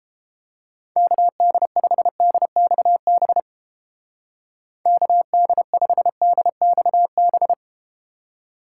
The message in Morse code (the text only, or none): KD5DXB KD5DXB